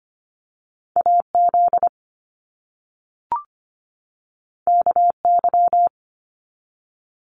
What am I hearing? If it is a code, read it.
Morse code, heard as a keyed tone, about 25 wpm, 700 Hz. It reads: AZ KY
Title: US States 25wpm